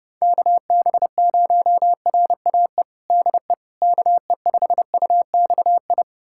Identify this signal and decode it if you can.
Morse code, heard as a keyed tone, about 30 wpm, 700 Hz. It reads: KB0RAE DE KE5UXI
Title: USA Callsign DE Callsign 30wpm